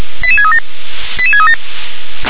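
Five-tone selective call (ZVEI), heard as quick runs of five tones, tones of about 70 ms. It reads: 88637 88637
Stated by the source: Five-tone ZVEI selective call 88747, received by SDR.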